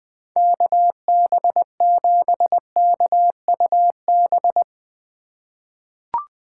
Morse code, heard as a keyed tone, about 21 wpm, 700 Hz. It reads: KB7KUB
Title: USA Call Signs - 4x Character Spacing - 20wpm